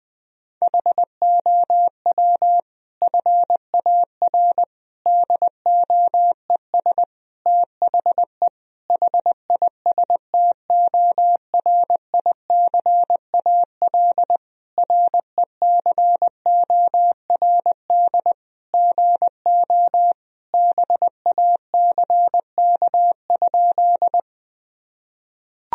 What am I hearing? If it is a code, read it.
Morse code, heard as a keyed tone, about 21 wpm, 700 Hz. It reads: HOW FAR DOES THE HISTORICAL RECORD GO BACK?
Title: Sentences from Top 1000 Words - Review Part 4 - 20wpm